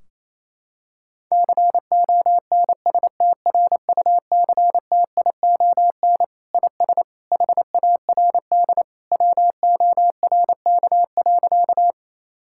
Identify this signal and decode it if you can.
Morse code, heard as a keyed tone, about 28 wpm, 700 Hz. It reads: CONSTRUCTION IS HARD WORK.
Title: Sentences from Top 1600 Words 28wpm